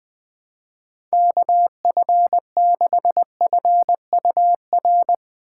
Morse code, heard as a keyed tone, about 20 wpm, 700 Hz. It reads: KF6FUR